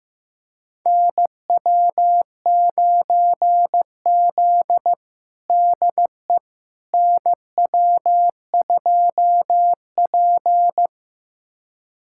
Morse code, heard as a keyed tone, about 15 wpm, 700 Hz. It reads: NW9Z DE NW2P
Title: USA Callsign DE Callsign - 4 Characters - 15wpm